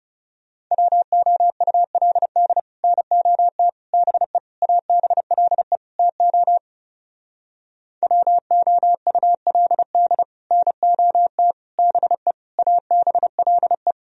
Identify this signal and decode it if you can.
Morse code, heard as a keyed tone, about 36 wpm, 700 Hz. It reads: WOULD NOT BE ABLE TO WOULD NOT BE ABLE
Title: Speed-Racing - Sets of 5 Words 30wpm